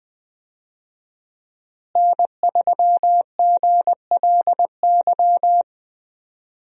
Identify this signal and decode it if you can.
Morse code, heard as a keyed tone, about 20 wpm, 700 Hz. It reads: N3GLY
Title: USA Call Signs - 5 Characters - Repeated 3x - 20wpm